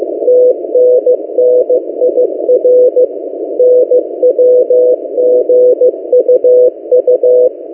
Morse code, heard as a keyed tone, about 15 wpm, 500 Hz. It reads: TNNIR NWGUU
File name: M08a- -1355utc-8096khz